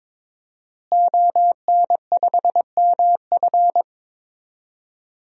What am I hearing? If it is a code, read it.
Morse code, heard as a keyed tone, about 22 wpm, 700 Hz. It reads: ON5MF
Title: DXCC-100 Call Signs - 5 Characters - Repeated 3x - 22wpm